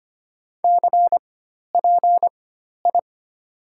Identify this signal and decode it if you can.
Morse code, heard as a keyed tone, about 25 wpm, spaced as for 12 wpm, 700 Hz. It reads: CPI
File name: Top 90 QSO Elements - 4x Character Spacing - 25wpm